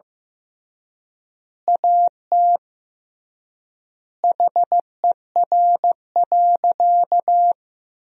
Morse code, heard as a keyed tone, about 15 wpm, 700 Hz. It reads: AT HER.
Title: 3x Word-Spacing - Sentences from Top 200 Words - 15wpm